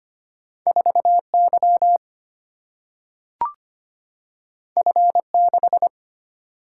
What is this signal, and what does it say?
Morse code, heard as a keyed tone, about 25 wpm, 700 Hz. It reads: 4 Y F 6
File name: Two Letter-Number Combinations - 25wpm